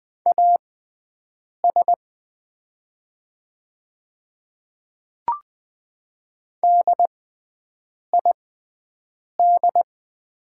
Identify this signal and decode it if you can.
Morse code, heard as a keyed tone, about 20 wpm, spaced as for 7 wpm, 690 Hz. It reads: AS DID
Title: 6x Character Spacing - Top 100 Words - 20wpm